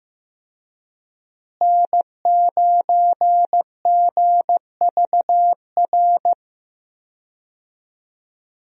Morse code, heard as a keyed tone, about 15 wpm, 700 Hz. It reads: N9GVR
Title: USA Call Signs - 5 Characters - Repeated 2x - 15wpm